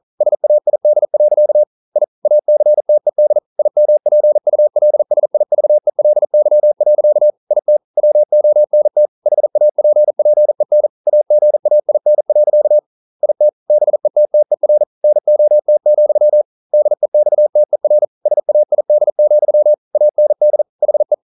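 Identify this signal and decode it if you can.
Morse code, heard as a keyed tone, about 41 wpm, 600 Hz. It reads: SAID. I ACTED IMPULSIVELY. IT WONT HAPPEN AGAIN. IT BETTER NOT, DEXTER SAID, AND HE
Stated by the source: Morse code practice at 41 wpm, from CW med Gnister.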